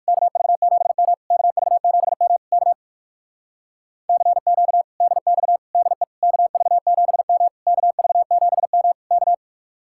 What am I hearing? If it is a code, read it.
Morse code, heard as a keyed tone, about 58 wpm, 700 Hz. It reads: KV7M KV7M K CQ DX DE KV7M KV7M K